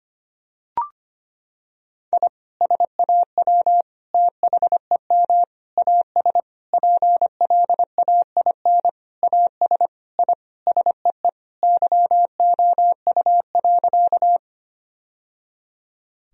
Morse code, heard as a keyed tone, about 25 wpm, 700 Hz. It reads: I SAW THEM AS PLAIN AS I SEE YOU.
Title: Sentences from Top 1000 Words - Review Part 2 - 25wpm